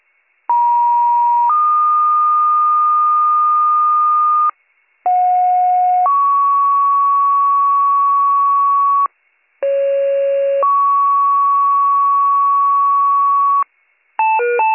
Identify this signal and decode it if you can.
Two-tone page, heard as a one-second tone, then a three-second tone, about 950 then 1220 Hz, about 730 then 1120 Hz, about 550 then 1120 Hz. It is heard from 0.5 to 13.6 s.